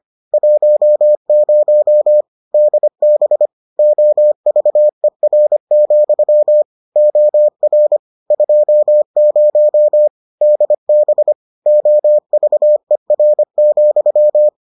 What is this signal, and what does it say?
Morse code, heard as a keyed tone, about 25 wpm, 600 Hz. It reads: 10 DB OVER, OR 20 DB OVER,